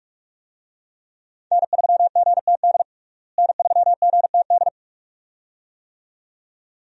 Morse code, heard as a keyed tone, about 41 wpm, 700 Hz. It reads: N3GTD N3GTD
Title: USA Call Signs - 5 Characters - Repeated 2x - 45wpm